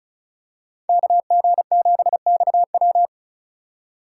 Morse code, heard as a keyed tone, about 34 wpm, 690 Hz. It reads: KG7XW
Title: USA Call Signs - 5 Characters - 3x Character Spacing - 35wpm